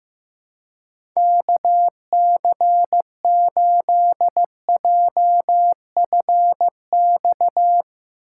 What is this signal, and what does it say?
Morse code, heard as a keyed tone, about 15 wpm, 700 Hz. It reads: KC8JFX
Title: USA Call Signs - 6 Characters - 4x Character Spacing - 15wpm